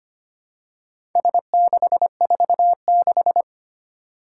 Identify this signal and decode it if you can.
Morse code, heard as a keyed tone, about 25 wpm, 700 Hz. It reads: S646
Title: Four Challenging Letter-Number Combinations 25wpm